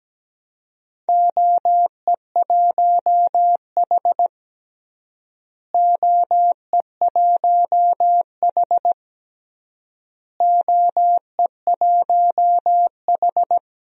Morse code, heard as a keyed tone, about 17 wpm, 700 Hz. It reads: OE1H OE1H OE1H
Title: DXCC-100 Call Signs - 4 Characters - Repeated 3x - 17wpm